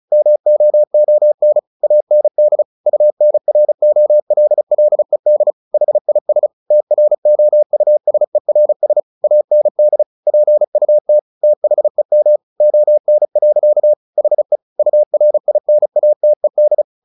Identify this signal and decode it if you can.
Morse code, heard as a keyed tone, about 35 wpm, 600 Hz. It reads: MOON AND UNROLLED HIS TROUSERS AND PUT THEM ON. HE URINATED
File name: the old man and the sea / the old man and the sea 1 35wpm